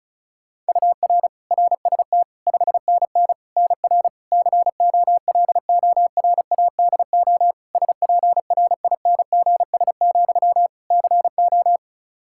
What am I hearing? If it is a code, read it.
Morse code, heard as a keyed tone, about 34 wpm, 700 Hz. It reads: UR RST 5NN NR COLORADO SPRINGS, CO
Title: Practice QSO - Exchange 1 - Beginning - 35wpm